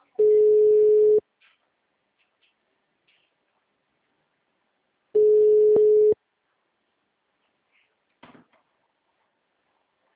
Ringback tone (Europe, 425 Hz).